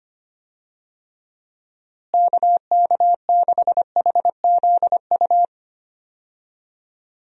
Morse code, heard as a keyed tone, about 25 wpm, 700 Hz. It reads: KK6HZU
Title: USA Call Signs - 6 Characters - Repeated 3x - 25wpm